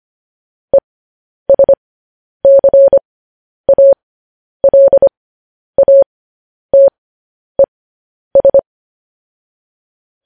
Morse code, heard as a keyed tone, about 25 wpm, spaced as for 10 wpm, 550 Hz. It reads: ESCALATES
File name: podcast / News headlines-1x-25x10